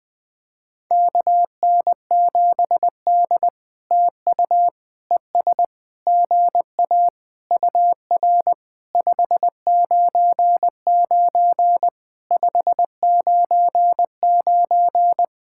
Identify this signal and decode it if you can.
Morse code, heard as a keyed tone, about 20 wpm, 700 Hz. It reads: KN7D TU ES GA UR 599 599
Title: POTA QSO Practice - 20wpm